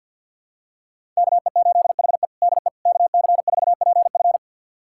Morse code, heard as a keyed tone, about 52 wpm, 690 Hz. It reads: KE8HE DE KC4PF
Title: USA Callsign DE Callsign - 5 Characters - 50wpm